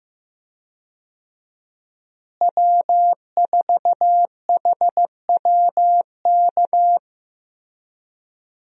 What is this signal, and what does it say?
Morse code, heard as a keyed tone, about 15 wpm, 700 Hz. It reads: W4HWK
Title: USA Call Signs - 5 Characters - Repeated 2x - 15wpm